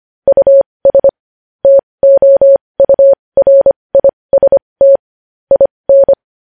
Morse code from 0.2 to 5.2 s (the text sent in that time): US TOURIST